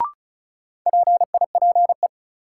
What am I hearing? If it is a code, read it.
Morse code, heard as a keyed tone, about 36 wpm, 700 Hz. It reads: PIPE
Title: Sentences from Top 1500 Words - New Word Drill - 35wpm